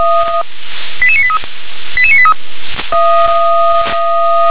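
Five-tone selective call (ZVEI), heard as quick runs of five tones, tones of about 70 ms. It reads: 88983 88983
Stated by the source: SDR-received ZVEI five-tone call 88791.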